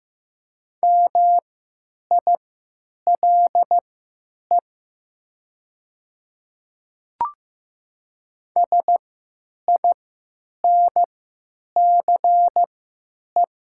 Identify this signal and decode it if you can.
Morse code, heard as a keyed tone, about 15 wpm, spaced as for 9 wpm, 700 Hz. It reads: MILE SINCE
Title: 3x Character Spacing - Top 300 Words - 15wpm